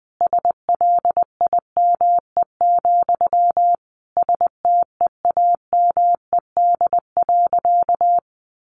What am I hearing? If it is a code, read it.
Morse code, heard as a keyed tone, about 20 wpm, 700 Hz. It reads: SLIME, STEAMED.
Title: A Sound Of Thunder 20wpm